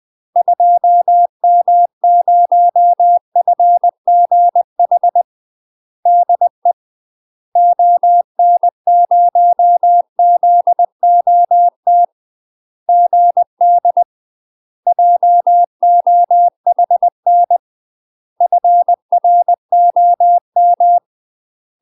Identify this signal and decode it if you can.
Morse code, heard as a keyed tone, about 20 wpm, 700 Hz. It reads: 2M0FGH DE ON0ZOT GD JOHN FROM
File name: UK MASTER QSOs TRAINING / 20WPM / UK MASTER QSOs 0064